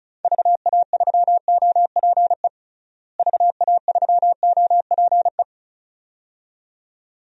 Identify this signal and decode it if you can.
Morse code, heard as a keyed tone, about 34 wpm, 700 Hz. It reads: VA3OPE VA3OPE
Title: DXCC-100 Call Signs - 6 Characters - Repeated 2x - 35wpm